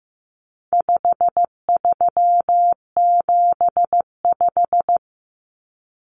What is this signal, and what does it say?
Morse code, heard as a keyed tone, about 15 wpm, 700 Hz. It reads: 5375
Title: Common Cut Numbers - 4-Digit 15wpm